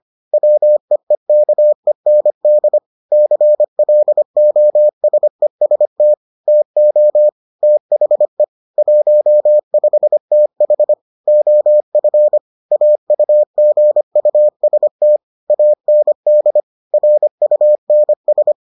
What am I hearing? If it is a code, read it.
Morse code, heard as a keyed tone, about 25 wpm, 600 Hz. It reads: WEEKEND CLOSEST TO THE 15TH OF AUGUST AND RUNS